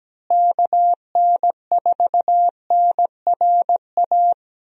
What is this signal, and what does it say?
Morse code, heard as a keyed tone, about 17 wpm, 700 Hz. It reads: KN4NRA
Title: USA Call Signs - 6 Characters - Repeated 3x - 17wpm